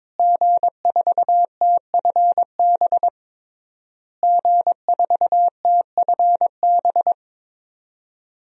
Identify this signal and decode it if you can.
Morse code, heard as a keyed tone, about 22 wpm, 700 Hz. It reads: G4TFB G4TFB
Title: DXCC-100 Call Signs - 5 Characters - Repeated 3x - 22wpm